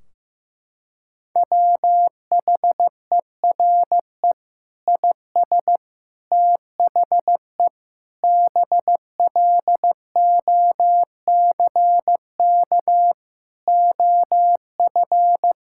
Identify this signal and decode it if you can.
Morse code, heard as a keyed tone, about 15 wpm, 700 Hz. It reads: WHERE IS THE BLOCK OF
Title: Sentences from Top 1600 Words 15wpm